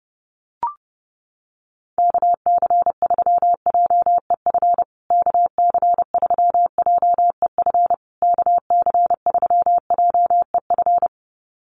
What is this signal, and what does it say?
Morse code, heard as a keyed tone, about 30 wpm, 700 Hz. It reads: KC3JEF KC3JEF KC3JEF
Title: USA Call Signs Repeated 3x 30wpm